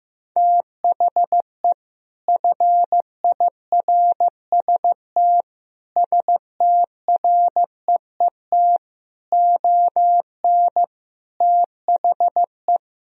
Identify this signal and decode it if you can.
Morse code, heard as a keyed tone, about 15 wpm, 700 Hz. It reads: THE FIRST STREET ON THE
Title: Sentences from Top 1000 Words - Review Part 3 - 15wpm